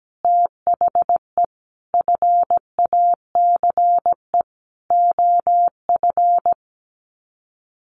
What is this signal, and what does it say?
Morse code, heard as a keyed tone, about 17 wpm, 700 Hz. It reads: THE FACE OF
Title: Sets of 4 Words 17wpm